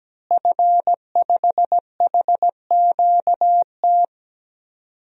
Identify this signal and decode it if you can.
Morse code, heard as a keyed tone, about 17 wpm, 700 Hz. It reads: F5HQT
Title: DXCC-100 Call Signs - 5 Characters - Repeated 2x - 17wpm